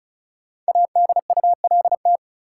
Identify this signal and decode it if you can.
Morse code, heard as a keyed tone, about 36 wpm, 700 Hz. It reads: ADULT
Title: Five Letter Words - 35wpm